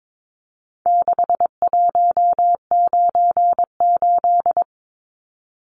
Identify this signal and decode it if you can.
Morse code, heard as a keyed tone, about 22 wpm, 700 Hz. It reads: 6198
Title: Numbers - 4-Digit 22wpm